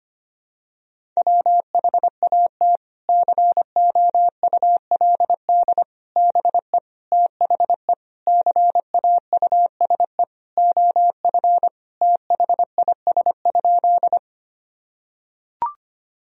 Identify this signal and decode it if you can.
Morse code, heard as a keyed tone, about 25 wpm, 700 Hz. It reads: WHAT COULD BE THE CAUSE OF THIS?
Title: Sentences from Top 1000 Words - Review Part 1 - 25wpm